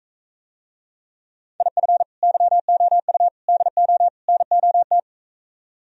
Morse code, heard as a keyed tone, about 45 wpm, 700 Hz. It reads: IF YOU DO NOT